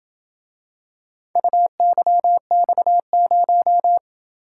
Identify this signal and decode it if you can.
Morse code, heard as a keyed tone, about 26 wpm, 700 Hz. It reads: UYX0